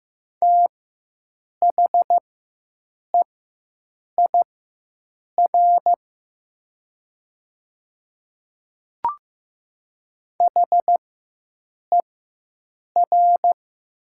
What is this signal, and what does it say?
Morse code, heard as a keyed tone, about 15 wpm, spaced as for 7 wpm, 700 Hz. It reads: THEIR HER
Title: 4x Character Spacing - Top 300 Words - 15wpm